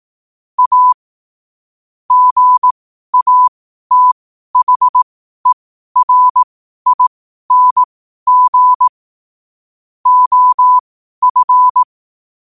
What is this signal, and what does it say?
Morse code, heard as a keyed tone, about 18 wpm, spaced as for 12 wpm, 1000 Hz. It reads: A GATHERING OF
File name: qotdfiles / 12wpm